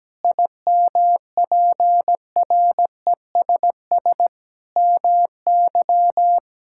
Morse code, heard as a keyed tone, about 17 wpm, 700 Hz. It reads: IMPRESS MY
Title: Sentences from Top 1600 Words 17wpm